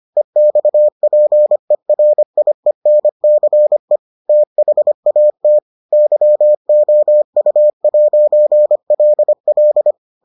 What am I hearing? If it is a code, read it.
Morse code, heard as a keyed tone, about 25 wpm, 600 Hz. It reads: EXPERIENCE THAT YOU'LL